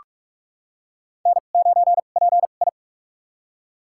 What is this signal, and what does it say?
Morse code, heard as a keyed tone, about 44 wpm, spaced as for 32 wpm, 700 Hz. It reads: N9PI
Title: USA Call Signs - 4 Characters - 2x Character Spacing - 45wpm